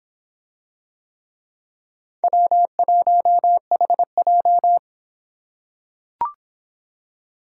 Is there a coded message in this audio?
Yes: Morse code from 2.2 to 4.8 s, heard as a keyed tone, about 24 wpm, 700 Hz. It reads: W1HJ